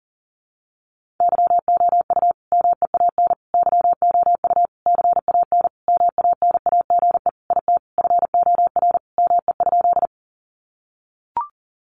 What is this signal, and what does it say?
Morse code, heard as a keyed tone, about 37 wpm, 700 Hz. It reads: YOU MEAN YOU CAN MANAGE IT FOR ME?
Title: Sentences from Top 1400 Words 40wpm